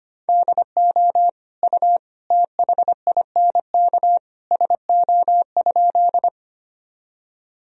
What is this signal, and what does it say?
Morse code, heard as a keyed tone, about 25 wpm, 700 Hz. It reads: DO U THINK SO?
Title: Sentences from Top 200 Words and Abbreviations 25wpm